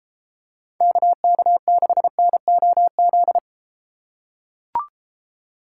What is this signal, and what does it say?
Morse code, heard as a keyed tone, about 34 wpm, 700 Hz. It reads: KK6NOZ